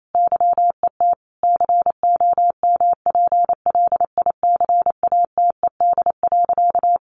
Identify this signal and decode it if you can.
Morse code, heard as a keyed tone, about 27 wpm, 700 Hz. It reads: YET COMPLICATED.
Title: Sentences from Top 1200 Words 28wpm